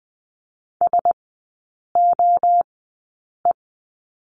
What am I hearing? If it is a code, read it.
Morse code, heard as a keyed tone, about 20 wpm, spaced as for 8 wpm, 700 Hz. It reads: SOE
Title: 053 Lesson 10e - Letter r - Three Characters 20wpm